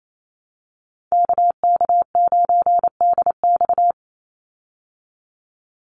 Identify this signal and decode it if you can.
Morse code, heard as a keyed tone, about 28 wpm, 700 Hz. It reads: KK9DX
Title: Call Signs Repeated 2x - 28wpm